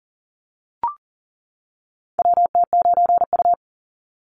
Morse code, heard as a keyed tone, about 40 wpm, 700 Hz. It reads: WT9U
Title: Call Signs - 40wpm